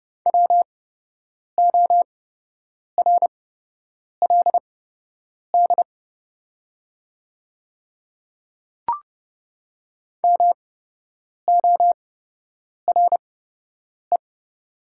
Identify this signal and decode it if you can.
Morse code, heard as a keyed tone, about 30 wpm, spaced as for 8 wpm, 700 Hz. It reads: WORLD MORE
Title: 8x Character Spacing - Top 500 Words - 30wpm